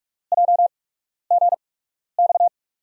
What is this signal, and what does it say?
Morse code, heard as a keyed tone, about 47 wpm, spaced as for 12 wpm, 700 Hz. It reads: JGX
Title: USA Call Signs - 6 Characters - 8x Character Spacing - 45wpm